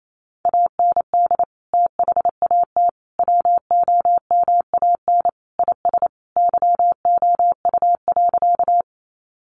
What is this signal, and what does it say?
Morse code, heard as a keyed tone, about 28 wpm, 700 Hz. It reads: AND THAT WOMAN IS YOU.
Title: Sentences from Top 1100 Words 28wpm